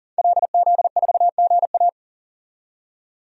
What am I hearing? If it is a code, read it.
Morse code, heard as a keyed tone, about 40 wpm, 700 Hz. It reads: LZ4GA
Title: DXCC-100 Call Signs - 5 Characters - 40wpm